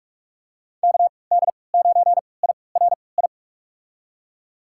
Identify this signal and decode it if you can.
Morse code, heard as a keyed tone, about 46 wpm, spaced as for 26 wpm, 690 Hz. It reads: KD9IRI
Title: USA Call Signs - 6 Characters - 3x Character Spacing - 45wpm